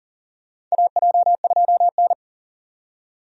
Morse code, heard as a keyed tone, about 40 wpm, 700 Hz. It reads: AJ2N
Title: USA Call Signs - 4 Characters - 40wpm